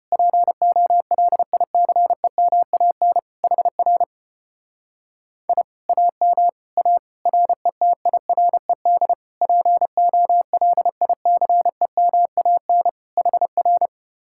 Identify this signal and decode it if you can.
Morse code, heard as a keyed tone, about 33 wpm, 700 Hz. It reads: POLICEMAN HR I AM A RETIRED POLICEMAN HR